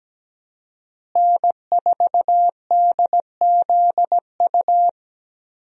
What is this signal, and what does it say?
Morse code, heard as a keyed tone, about 17 wpm, 700 Hz. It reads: N4DZU
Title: USA Call Signs - 5 Characters - Repeated 2x - 17wpm